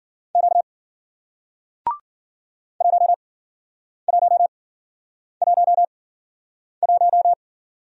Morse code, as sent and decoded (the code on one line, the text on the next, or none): --.- / .---- .---- .---- .----
Q 1111